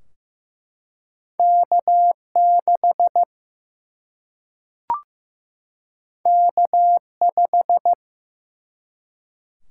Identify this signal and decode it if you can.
Morse code, heard as a keyed tone, about 15 wpm, 700 Hz. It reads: K6 K5
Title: Single-Letter Single-Number 15wpm